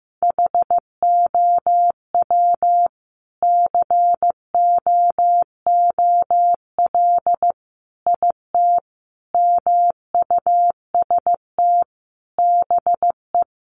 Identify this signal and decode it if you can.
Morse code, heard as a keyed tone, about 14 wpm, 700 Hz. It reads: HOW COOL IT MUST BE
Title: Sentences from Top 1200 Words 15wpm